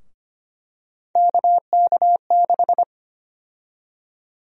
Morse code, heard as a keyed tone, about 25 wpm, 700 Hz. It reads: KK6
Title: USA Call Sign Prefixes - Repeated 2x - 25wpm